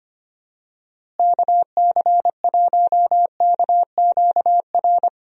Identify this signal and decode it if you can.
Morse code, heard as a keyed tone, about 25 wpm, 700 Hz. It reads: KC1KQR
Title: USA Call Signs - 6 Characters - Repeated 2x - 25wpm